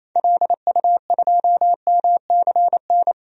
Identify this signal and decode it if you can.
Morse code, heard as a keyed tone, about 28 wpm, 700 Hz. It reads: LU2MCN
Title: DXCC-100 Call Signs - 6 Characters - 28wpm